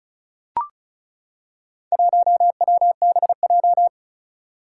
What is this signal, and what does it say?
Morse code, heard as a keyed tone, about 36 wpm, 700 Hz. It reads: 1WBJ
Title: Four Challenging Letter-Number Combinations 35wpm